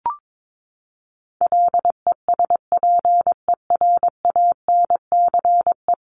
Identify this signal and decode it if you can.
Morse code, heard as a keyed tone, about 22 wpm, 700 Hz. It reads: LESPERANCE
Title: A Sound Of Thunder 22wpm